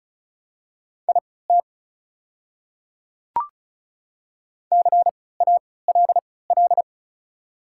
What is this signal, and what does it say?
Morse code, heard as a keyed tone, about 36 wpm, spaced as for 20 wpm, 700 Hz. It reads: IT CALL
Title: 3x Character Spacing - Top 100 Words - 35wpm